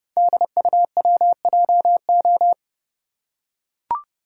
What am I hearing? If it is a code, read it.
Morse code, heard as a keyed tone, about 30 wpm, 700 Hz. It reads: DUWJO
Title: Code Groups - 5 Letter:Numbers - 30wpm